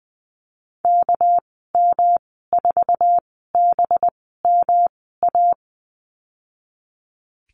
Morse code, heard as a keyed tone, about 20 wpm, spaced as for 14 wpm, 700 Hz. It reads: KM4BMA
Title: 213 Lesson 34g - Number 8 - Call Signs 20wpm